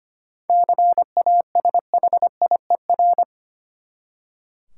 Morse code, heard as a keyed tone, about 25 wpm, 700 Hz. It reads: CASHIER
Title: Practice QSO - Exchange 3 - More Info - 25wpm